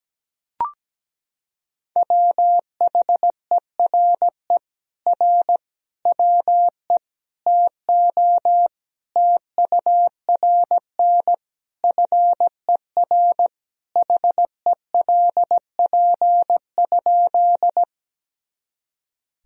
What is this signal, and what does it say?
Morse code, heard as a keyed tone, about 17 wpm, 700 Hz. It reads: WHERE R WE TO TURN FER HELP?
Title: Sentences from Top 200 Words and Abbreviations 17wpm